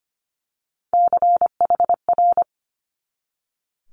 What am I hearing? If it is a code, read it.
Morse code, heard as a keyed tone, about 25 wpm, 700 Hz. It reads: CHR
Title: Multi-letter Phonemes 25wpm